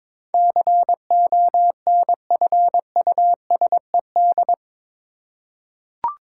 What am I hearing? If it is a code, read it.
Morse code, heard as a keyed tone, about 21 wpm, 700 Hz. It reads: CONFUSED
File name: practice / 22wpm / Eight Letter Words - 22wpm